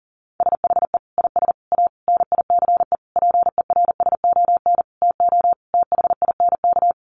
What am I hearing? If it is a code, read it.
Morse code, heard as a keyed tone, about 37 wpm, 700 Hz. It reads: SHE IS A NICE PERSON TO THINK
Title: Sentences from Top 1100 Words 40wpm